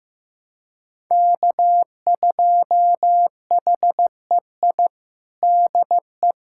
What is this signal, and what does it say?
Morse code, heard as a keyed tone, about 15 wpm, 700 Hz. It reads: K2HEI DE
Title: USA Callsign DE Callsign - 5 Characters - 15wpm